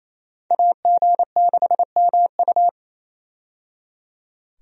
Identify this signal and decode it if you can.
Morse code, heard as a keyed tone, about 28 wpm, 700 Hz. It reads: AG6MU
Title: USA Call Signs - 5 Characters - 28wpm